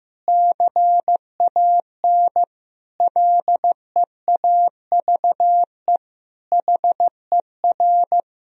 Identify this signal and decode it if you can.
Morse code, heard as a keyed tone, about 15 wpm, 700 Hz. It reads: CAN LEAVE HER
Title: Sentences from Top 1000 Words - Review Part 3 - 15wpm